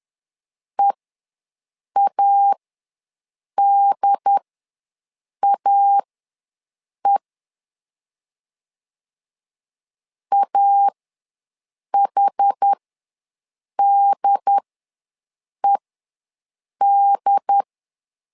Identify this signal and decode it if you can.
Morse code, heard as a keyed tone, about 10 wpm, spaced as for 6 wpm, 800 Hz. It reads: EADAE AHDED
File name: LEZIONI / 2 UHDN / 2 UHD ws 10wpm 2058